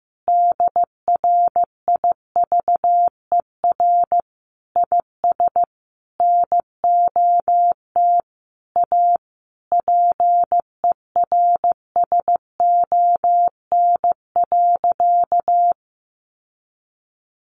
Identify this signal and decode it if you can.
Morse code, heard as a keyed tone, about 15 wpm, 700 Hz. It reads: DRIVER IS NOT A PERSON.
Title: Sentences from Top 1500 Words 15wpm